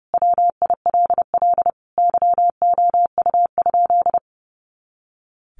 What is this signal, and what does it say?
Morse code, heard as a keyed tone, about 30 wpm, 700 Hz. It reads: WILL YOU?
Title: Sentences from Top 300 Words 30wpm